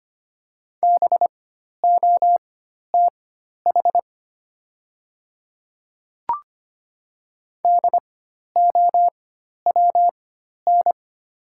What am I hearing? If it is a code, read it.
Morse code, heard as a keyed tone, about 25 wpm, spaced as for 12 wpm, 700 Hz. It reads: BOTH DOWN
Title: 4x Character Spacing - Top 300 Words - 25wpm